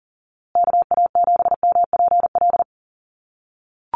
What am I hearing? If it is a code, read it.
Morse code, heard as a keyed tone, about 39 wpm, 700 Hz. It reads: KA7MPL
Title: USA Call Signs Repeated 2x 40wpm